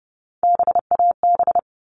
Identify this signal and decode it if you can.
Morse code, heard as a keyed tone, about 30 wpm, 700 Hz. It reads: BAB
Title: All Cut Numbers - 3-Digit 30wpm